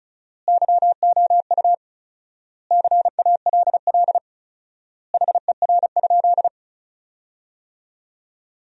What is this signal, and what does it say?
Morse code, heard as a keyed tone, about 34 wpm, 700 Hz. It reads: YOU CALL HER?
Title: 4x Word-Spacing - Sentences from Top 100 Words - 35wpm